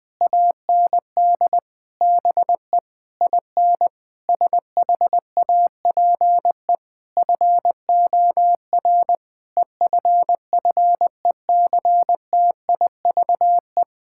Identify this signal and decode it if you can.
Morse code, heard as a keyed tone, about 20 wpm, 700 Hz. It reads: AND BE IN SHAPE FOR EFFECTIVE
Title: Sentences from Top 1000 Words - Review Part 6 - 20wpm